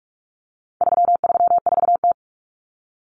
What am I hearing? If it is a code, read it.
Morse code, heard as a keyed tone, about 44 wpm, 700 Hz. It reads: 334T
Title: Common Cut Numbers - 4-Digit 45wpm